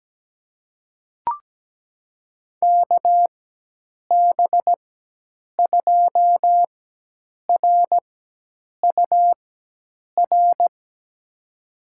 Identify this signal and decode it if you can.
Morse code, heard as a keyed tone, about 17 wpm, spaced as for 8 wpm, 700 Hz. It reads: KB2RUR
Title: USA Call Signs - 6 Characters - 4x Character Spacing - 17wpm